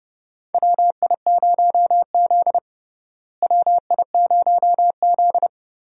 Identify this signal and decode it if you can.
Morse code, heard as a keyed tone, about 30 wpm, 700 Hz. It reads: WI0Z WI0Z
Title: USA Call Signs - 4 Characters - Repeated 3x - 30wpm